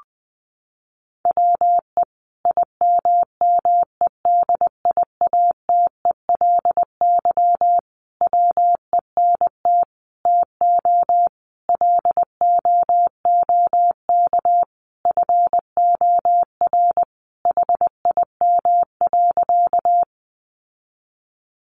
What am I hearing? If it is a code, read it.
Morse code, heard as a keyed tone, about 20 wpm, 700 Hz. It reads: WE IMMEDIATELY WENT TO LOOK FOR HIM.
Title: Sentences from Top 1100 Words 20wpm